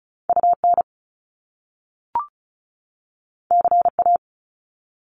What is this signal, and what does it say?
Morse code, heard as a keyed tone, about 36 wpm, 700 Hz. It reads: UN CA
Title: Top Two Letter Combinations in Words 35wpm